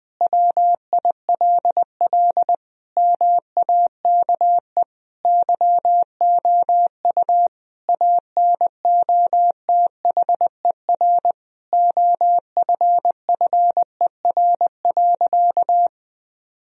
Morse code, heard as a keyed tone, about 20 wpm, 700 Hz. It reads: WILL MAKE YOU ANOTHER OFFER.
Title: Sentences from Top 1000 Words - Review Part 4 - 20wpm